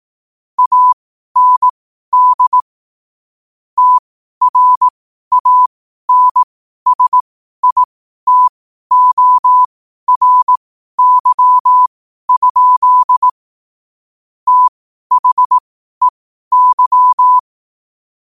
Morse code, heard as a keyed tone, about 18 wpm, spaced as for 12 wpm, 1000 Hz. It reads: AND TRANSITORY? THEY
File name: qotdfiles / 12wpm